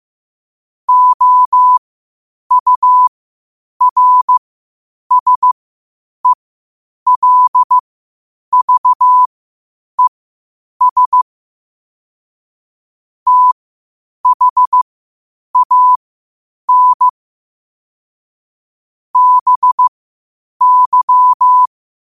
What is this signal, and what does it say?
Morse code, heard as a keyed tone, about 15 wpm, spaced as for 8 wpm, 1000 Hz. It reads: OURSELVES THAN BY
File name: qotdfiles / 8wpm